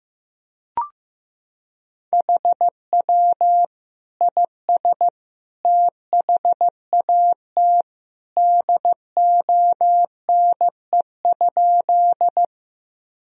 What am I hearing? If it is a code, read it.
Morse code, heard as a keyed tone, about 16 wpm, 700 Hz. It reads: HW IS THAT DONE?
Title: Sentences from Top 500 Words and Abbreviations 15wpm